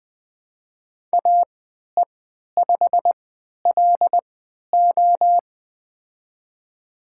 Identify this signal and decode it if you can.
Morse code, heard as a keyed tone, about 20 wpm, spaced as for 11 wpm, 700 Hz. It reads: AE5LO
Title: USA Call Signs - 3x Character Spacing - 20wpm